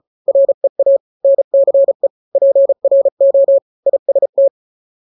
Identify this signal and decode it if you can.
Morse code, heard as a keyed tone, about 34 wpm, 550 Hz. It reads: REA NCE PRO IST
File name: 3-grams 35 30 WPM